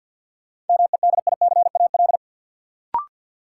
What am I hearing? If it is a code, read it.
Morse code, heard as a keyed tone, about 48 wpm, 700 Hz. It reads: MEDICAL
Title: Sentences from Top 800 Words - New Word Drill - 50wpm